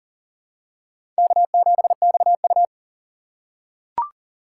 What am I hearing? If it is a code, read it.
Morse code, heard as a keyed tone, about 40 wpm, 700 Hz. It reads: K7XU
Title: USA Call Signs - 4 Characters - 3x Character Spacing - 40wpm